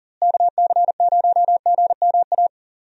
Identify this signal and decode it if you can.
Morse code, heard as a keyed tone, about 40 wpm, 700 Hz. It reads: KC0GMA
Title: USA Call Signs - 6 Characters - Repeated 2x - 40wpm